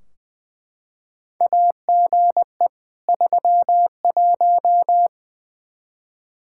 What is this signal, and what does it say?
Morse code, heard as a keyed tone, about 20 wpm, 700 Hz. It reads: AGE 31
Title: Practice QSO - Exchange 3 - More Info - 20wpm